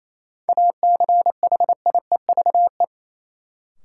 Morse code, heard as a keyed tone, about 28 wpm, 700 Hz. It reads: ACHIEVE
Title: Sentences from Top 1000 Words - Review Part 5 - New Word Drill - 28wpm